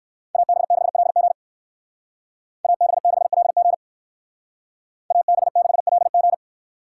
Morse code, heard as a keyed tone, about 62 wpm, 700 Hz. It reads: AB6LG AB6LG AB6LG